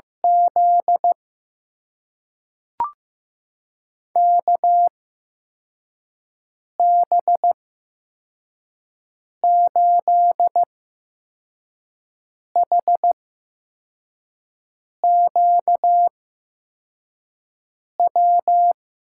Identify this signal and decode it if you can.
Morse code, heard as a keyed tone, about 15 wpm, spaced as for 4 wpm, 700 Hz. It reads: Z KB8HQW
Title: USA Call Signs - 6 Characters - 8x Character Spacing - 15wpm